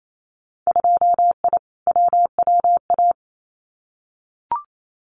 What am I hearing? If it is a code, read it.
Morse code, heard as a keyed tone, about 28 wpm, 700 Hz. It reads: 2I WWA
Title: Winter Field Day Exchanges Encore 28wpm